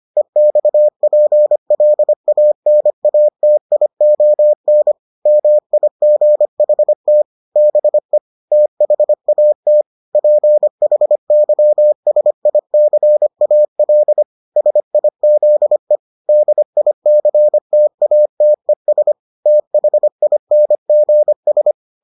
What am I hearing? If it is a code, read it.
Morse code, heard as a keyed tone, about 25 wpm, 600 Hz. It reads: EXPLANATION MIGHT BE THAT PHYSICAL SIZE DICTATES THINGS